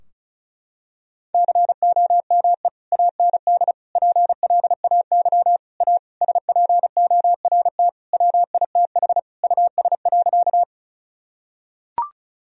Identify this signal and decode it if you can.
Morse code, heard as a keyed tone, about 36 wpm, 700 Hz. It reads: COME AND PLAY A SPORT WITH US.
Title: Sentences from Top 1000 Words - Review Part 4 - 35wpm